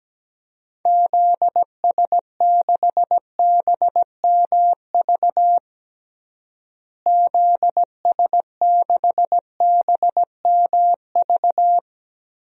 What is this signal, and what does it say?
Morse code, heard as a keyed tone, about 17 wpm, 700 Hz. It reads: ZS6BMV ZS6BMV
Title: DXCC-100 Call Signs - 6 Characters - Repeated 3x - 17wpm